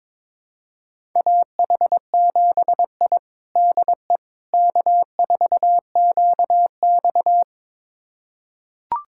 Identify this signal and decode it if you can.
Morse code, heard as a keyed tone, about 22 wpm, 700 Hz. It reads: AH7I DE K4QX
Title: USA Callsign DE Callsign - 4 Characters - 22wpm